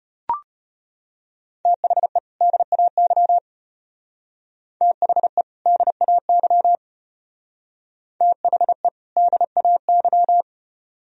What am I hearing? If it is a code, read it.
Morse code, heard as a keyed tone, about 32 wpm, 700 Hz. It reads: THE DAY THE DAY THE DAY